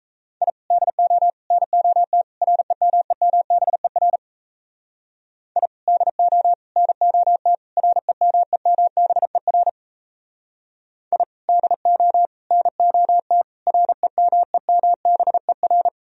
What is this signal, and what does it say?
Morse code, heard as a keyed tone, about 35 wpm, 700 Hz. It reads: I DO NOT REMEMBER I DO NOT REMEMBER I DO NOT REMEMBER
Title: Speed-Racing - Sets of 4 Words 28wpm